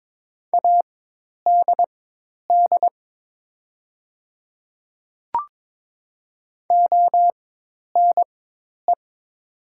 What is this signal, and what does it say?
Morse code, heard as a keyed tone, about 23 wpm, spaced as for 10 wpm, 700 Hz. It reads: ADD ONE
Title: 4x Character Spacing - Top 500 Words - 22wpm